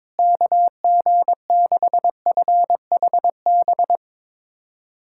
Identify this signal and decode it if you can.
Morse code, heard as a keyed tone, about 22 wpm, 700 Hz. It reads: KG6FHB
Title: USA Callsign DE Callsign - 6 Characters - 22wpm